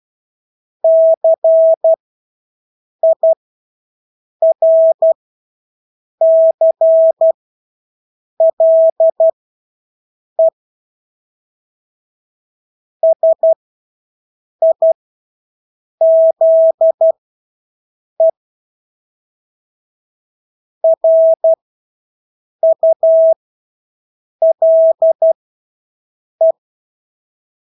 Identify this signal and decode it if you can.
Morse code, heard as a keyed tone, about 12 wpm, spaced as for 6 wpm, 650 Hz. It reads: CIRCLE SIZE RULE
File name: Engelske ord 12-6 wpm